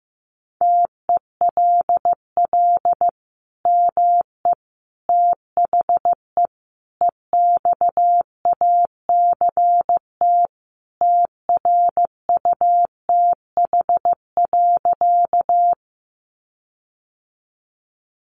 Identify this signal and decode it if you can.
Morse code, heard as a keyed tone, about 15 wpm, 700 Hz. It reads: TELL ME THE EXACT TRUTH.
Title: Sentences from Top 1300 Words 15wpm